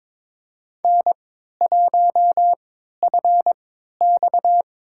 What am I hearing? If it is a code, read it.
Morse code, heard as a keyed tone, about 22 wpm, spaced as for 12 wpm, 700 Hz. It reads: N1FX
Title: USA Call Signs - 4 Characters - 3x Character Spacing - 22wpm